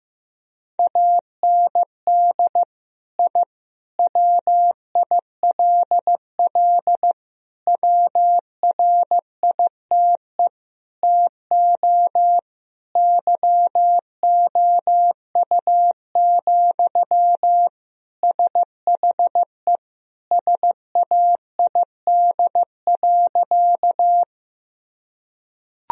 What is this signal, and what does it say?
Morse code, heard as a keyed tone, about 15 wpm, 700 Hz. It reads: AND I WILL WRITE TO YOU, SHE SAID.
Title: Sentences from Top 1000 Words - Review Part 1 - 15wpm